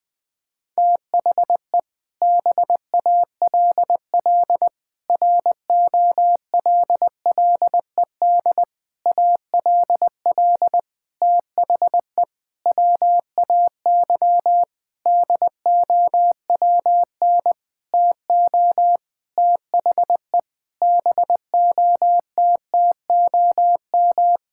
Morse code, heard as a keyed tone, about 20 wpm, 700 Hz. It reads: THE BALL ROLLED ALL THE WAY DOWN TO THE BOTTOM
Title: Sentences with 6-Word Phrases - 20wpm